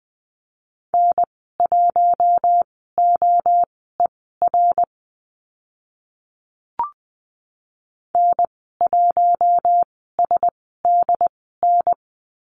Morse code, heard as a keyed tone, about 20 wpm, spaced as for 14 wpm, 700 Hz. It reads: N1OER N1SDN
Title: 082 Lesson 14g - Number 2 - Call Signs 20wpm